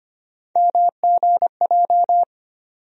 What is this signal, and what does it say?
Morse code, heard as a keyed tone, about 25 wpm, 700 Hz. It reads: MGJ
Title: USA Call Sign Postfixes - 3 Letters 25wpm